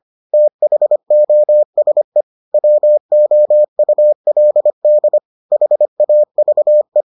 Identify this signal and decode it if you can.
Morse code, heard as a keyed tone, about 25 wpm, 600 Hz. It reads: THOSE WOULD HAVE